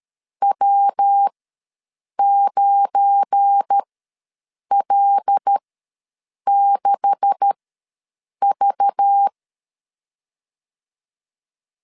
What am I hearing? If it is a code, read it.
Morse code, heard as a keyed tone, about 13 wpm, spaced as for 7 wpm, 800 Hz. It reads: W9L6V
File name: LEZIONI / 9 2867 ws 12wpm 3431